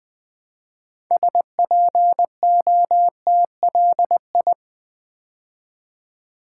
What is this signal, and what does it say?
Morse code, heard as a keyed tone, about 20 wpm, 700 Hz. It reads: SPOTLI
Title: Build A Word - 3 Letters At A Time - 9-Letter Words - 20wpm